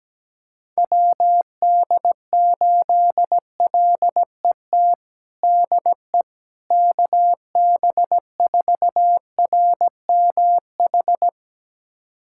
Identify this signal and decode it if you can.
Morse code, heard as a keyed tone, about 17 wpm, 700 Hz. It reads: WD8LET DE KB4RMH
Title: USA Callsign DE Callsign 17wpm